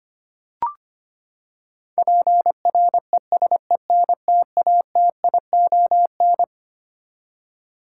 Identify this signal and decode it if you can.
Morse code, heard as a keyed tone, about 25 wpm, 700 Hz. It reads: PRESENTATION
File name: Twelve Letter Words - 25wpm